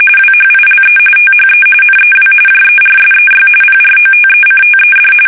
RTTY, about 45.45 baud. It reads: WHILE THE NEW JERSEY PROTECTED
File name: fsk850 / battleshipnj